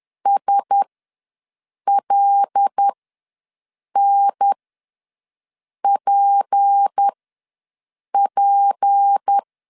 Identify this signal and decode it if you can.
Morse code, heard as a keyed tone, about 10 wpm, spaced as for 6 wpm, 800 Hz. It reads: SLNPP
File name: LEZIONI / 6 PKX0 / 6 P ws 10wpm 3751